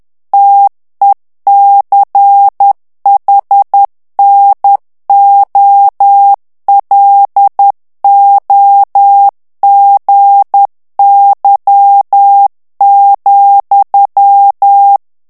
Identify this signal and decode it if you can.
Morse code, heard as a keyed tone, about 10 wpm, 800 Hz. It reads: TECHNOLOGY,
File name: ANGLAIS texte clair 3 vitesse 10 mots minute